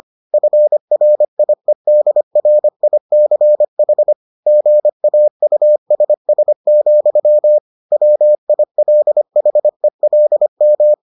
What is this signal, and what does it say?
Morse code, heard as a keyed tone, about 25 wpm, 600 Hz. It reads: FRIEDRICH GAUSS, WILHELM